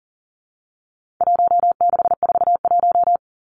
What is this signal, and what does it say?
Morse code, heard as a keyed tone, about 40 wpm, 700 Hz. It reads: 1641
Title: Numbers - Contest Style - 4-Digit 40wpm